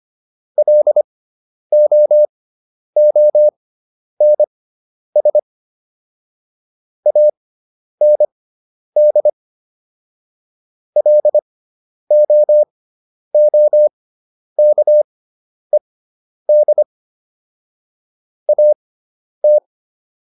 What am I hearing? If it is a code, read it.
Morse code, heard as a keyed tone, about 25 wpm, spaced as for 10 wpm, 600 Hz. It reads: LOONS AND LOOKED AT